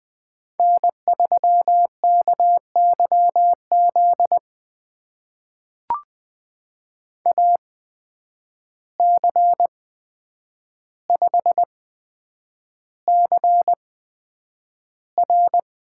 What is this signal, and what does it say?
Morse code, heard as a keyed tone, about 20 wpm, spaced as for 9 wpm, 700 Hz. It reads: N3KYZ AC5CR
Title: USA Call Signs - 5 Characters - 8x Character Spacing - 20wpm